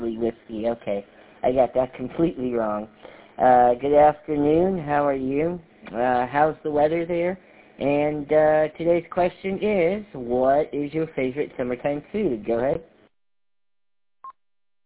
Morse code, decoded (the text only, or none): none